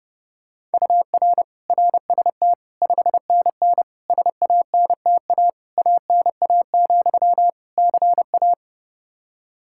Morse code, heard as a keyed tone, about 30 wpm, 700 Hz. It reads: UR RST 5NN SANTA ANA, CA
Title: Practice QSO - Exchange 1 - Beginning - 30wpm